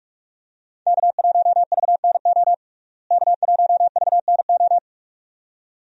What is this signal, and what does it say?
Morse code, heard as a keyed tone, about 46 wpm, 700 Hz. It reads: K1VNO K1VNO
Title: USA Call Signs - 5 Characters - Repeated 2x - 45wpm